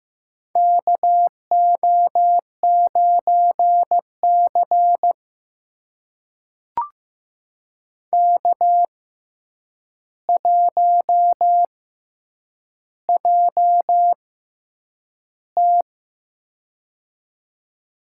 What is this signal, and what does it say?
Morse code, heard as a keyed tone, about 15 wpm, spaced as for 8 wpm, 700 Hz. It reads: KO9C K1JT
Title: USA Call Signs - 4 Characters - 6x Character Spacing - 15wpm